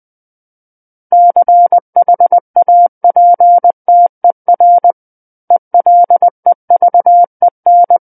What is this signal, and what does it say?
Morse code, heard as a keyed tone, about 20 wpm, 700 Hz. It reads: CHAPTER ELEVEN
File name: 20 WPM